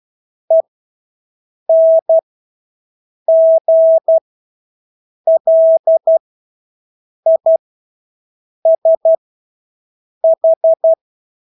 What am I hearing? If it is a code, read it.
Morse code, heard as a keyed tone, about 12 wpm, spaced as for 6 wpm, 650 Hz. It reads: ENGLISH
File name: Engelske ord 12-6 wpm